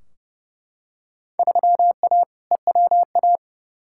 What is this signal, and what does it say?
Morse code, heard as a keyed tone, about 32 wpm, 700 Hz. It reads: 3A EWA
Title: Field Day Exchanges Encore - 30wpm